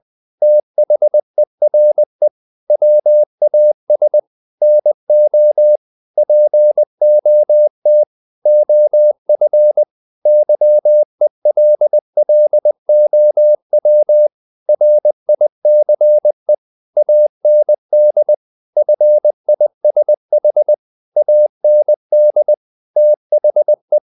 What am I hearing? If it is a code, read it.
Morse code, heard as a keyed tone, about 20 wpm, 600 Hz. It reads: THERE WAS NO POT OF YELLOW RICE AND FISH AND THE
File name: the old man and the sea / the old man and the sea part 1 20wpm